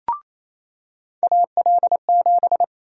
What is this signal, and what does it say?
Morse code, heard as a keyed tone, about 27 wpm, 700 Hz. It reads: AL7
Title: USA Call Sign Prefixes - 28wpm